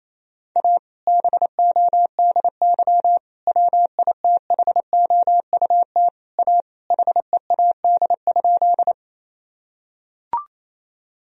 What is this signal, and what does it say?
Morse code, heard as a keyed tone, about 27 wpm, 700 Hz. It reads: A BODY WITHOUT A HEAD?
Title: Sentences from Top 1000 Words - Review Part 3 - 28wpm